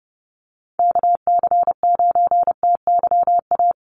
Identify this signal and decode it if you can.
Morse code, heard as a keyed tone, about 30 wpm, 700 Hz. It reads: KC9TYA
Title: NA SOTA Chaser Callsigns 30wpm